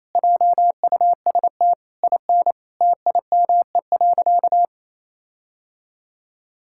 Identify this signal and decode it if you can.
Morse code, heard as a keyed tone, about 27 wpm, 700 Hz. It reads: JUST IN TIME.
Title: Sentences from Top 1600 Words 28wpm